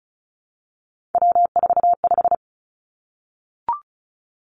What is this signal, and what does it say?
Morse code, heard as a keyed tone, about 36 wpm, 700 Hz. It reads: W45
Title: Three Challenging Letter-Number Combinations 35wpm